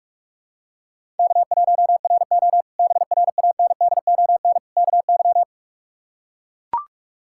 Morse code, heard as a keyed tone, about 44 wpm, 700 Hz. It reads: K1RO BRANDON KY
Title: K1USN SST QSO Practice - 45wpm